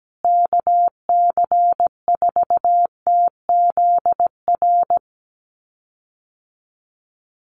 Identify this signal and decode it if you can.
Morse code, heard as a keyed tone, about 17 wpm, 700 Hz. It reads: KC4TZR
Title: USA Call Signs Repeated 3x 17wpm